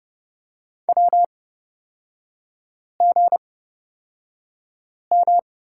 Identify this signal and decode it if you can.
Morse code, heard as a keyed tone, about 31 wpm, spaced as for 5 wpm, 700 Hz. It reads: WGM
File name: Lesson 38e - Letter q - Three Characters 30wpm